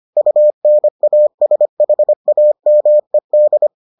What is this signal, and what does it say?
Morse code, heard as a keyed tone, about 25 wpm, 600 Hz. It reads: UNASHAMED